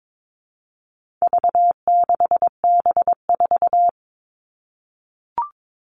Morse code, heard as a keyed tone, about 22 wpm, 700 Hz. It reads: V6B4
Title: All Cut Numbers - 4-Digit 22wpm